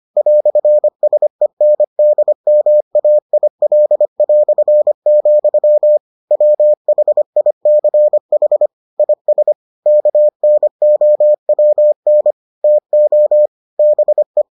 Morse code, heard as a keyed tone, about 25 wpm, 600 Hz. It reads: "SENDMAIL", WHICH IS KNOWN TO BE